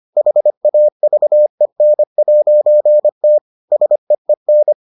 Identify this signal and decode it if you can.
Morse code, heard as a keyed tone, about 25 wpm, 600 Hz. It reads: HAVEN'T SEEN